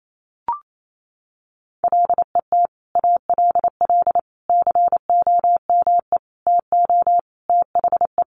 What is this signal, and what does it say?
Morse code, heard as a keyed tone, about 28 wpm, 700 Hz. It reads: LET ALL COME TO THE
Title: Sentences from Top 1100 Words 28wpm